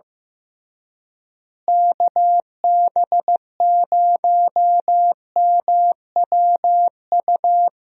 Morse code, heard as a keyed tone, about 15 wpm, 700 Hz. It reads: KB0MWU
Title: USA Call Signs - 6 Characters - Repeated 3x - 15wpm